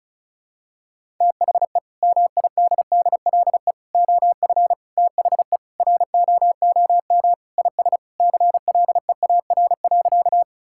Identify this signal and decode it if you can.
Morse code, heard as a keyed tone, about 36 wpm, 700 Hz. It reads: THE MIDDLE OF THE ROOM IS CLEAR.
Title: Sentences from Top 1000 Words - Review Part 6 - 35wpm